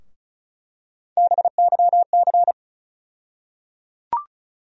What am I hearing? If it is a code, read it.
Morse code, heard as a keyed tone, about 36 wpm, 700 Hz. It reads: BYC
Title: USA Call Sign Postfixes - 3 Letters 35wpm